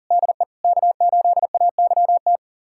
Morse code, heard as a keyed tone, about 40 wpm, 700 Hz. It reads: DE K8AYT
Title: USA Callsign DE Callsign 40wpm